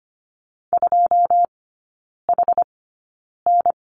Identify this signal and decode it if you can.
Morse code, heard as a keyed tone, about 25 wpm, spaced as for 9 wpm, 700 Hz. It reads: 2HN
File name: course / 25:10wpm / 093 Lesson 16e - Letter c - Three Characters 25wpm